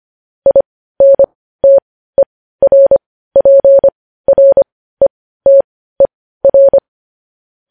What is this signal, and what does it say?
Morse code, heard as a keyed tone, about 25 wpm, spaced as for 15 wpm, 550 Hz. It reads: INTERPRETER